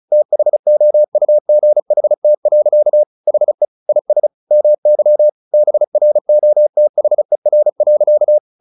Morse code, heard as a keyed tone, about 34 wpm, 600 Hz. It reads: THOUGHT. HE IS MY BROTHER.
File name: the old man and the sea / the old man and the sea 1 35wpm